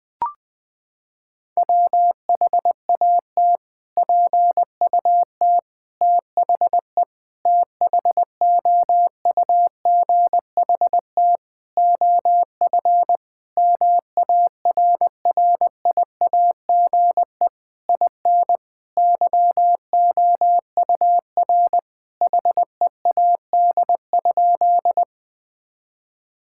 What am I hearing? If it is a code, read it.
Morse code, heard as a keyed tone, about 20 wpm, 700 Hz. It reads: WHAT PUT THE THOUGHT OF MARRIAGE IN YOUR HEAD?
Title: Sentences from Top 1000 Words - Review Part 5 - 20wpm